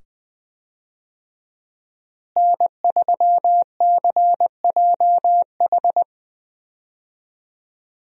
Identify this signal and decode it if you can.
Morse code, heard as a keyed tone, about 20 wpm, 700 Hz. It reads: N3CJH
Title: USA Call Signs - 5 Characters - Repeated 2x - 20wpm